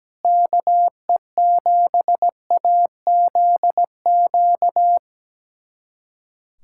Morse code, heard as a keyed tone, about 17 wpm, 700 Hz. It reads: KE7AZQ
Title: USA Callsign DE Callsign - 6 Characters - 17wpm